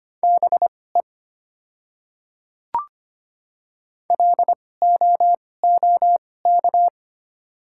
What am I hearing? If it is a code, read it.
Morse code, heard as a keyed tone, about 25 wpm, spaced as for 18 wpm, 690 Hz. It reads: BE LOOK
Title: 2x Character Spacing - Top 100 Words - 25wpm